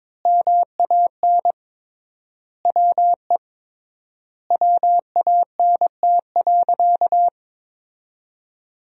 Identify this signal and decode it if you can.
Morse code, heard as a keyed tone, about 22 wpm, 700 Hz. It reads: MAN WE WANT.
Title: 3x Word-Spacing - Sentences from Top 200 Words - 22wpm